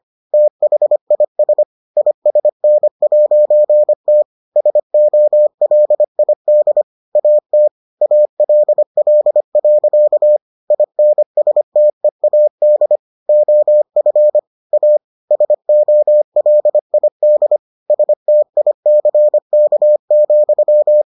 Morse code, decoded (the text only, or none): THIS ISN'T SOLID AT ALL. INSTEAD OF A SOLID STICK,